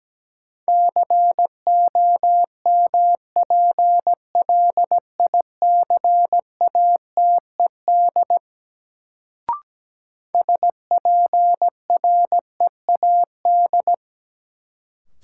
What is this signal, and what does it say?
Morse code, heard as a keyed tone, about 17 wpm, 700 Hz. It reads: COMPLICATED SPREAD
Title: Sentences from Top 1200 Words - New Word Drill - 17wpm